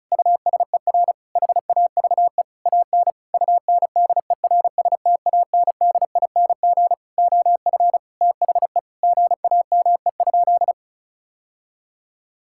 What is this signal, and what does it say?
Morse code, heard as a keyed tone, about 34 wpm, 700 Hz. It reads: USER HAVE AN UNDERSTANDING OF THE GAME?
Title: Sentences from Top 1000 Words - Review Part 5 - 35wpm